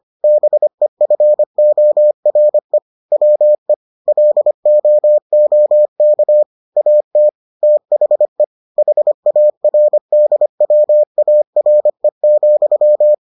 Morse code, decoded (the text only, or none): BEFORE WE LOOK AT THE HARDWARE,